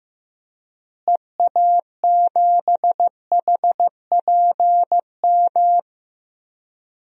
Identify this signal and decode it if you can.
Morse code, heard as a keyed tone, about 16 wpm, 700 Hz. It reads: EA7HPM
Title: DXCC-100 Call Signs - 6 Characters - Repeated 3x - 15wpm